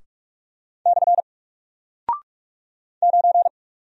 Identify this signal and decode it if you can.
Morse code, heard as a keyed tone, about 44 wpm, 700 Hz. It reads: / 9
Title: Single Character (Letter:Number:Punctuation:Prosign) 45wpm